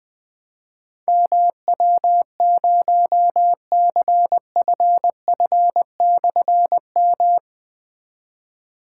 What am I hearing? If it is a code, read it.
Morse code, heard as a keyed tone, about 20 wpm, 700 Hz. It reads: MW0CFF/M
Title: Difficult Call Signs 20wpm